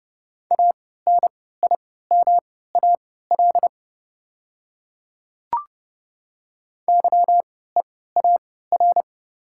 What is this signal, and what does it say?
Morse code, heard as a keyed tone, about 30 wpm, spaced as for 17 wpm, 690 Hz. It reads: ANIMAL YEAR
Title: 3x Character Spacing - Top 300 Words - 30wpm